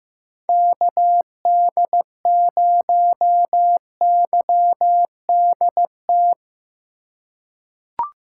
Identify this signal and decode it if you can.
Morse code, heard as a keyed tone, about 15 wpm, 700 Hz. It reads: KD0YDT
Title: USA Call Signs - 6 Characters - 4x Character Spacing - 15wpm